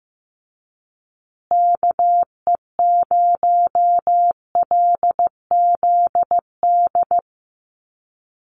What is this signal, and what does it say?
Morse code, heard as a keyed tone, about 15 wpm, 700 Hz. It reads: KE0LZD
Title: USA Call Signs Repeated 2x 15wpm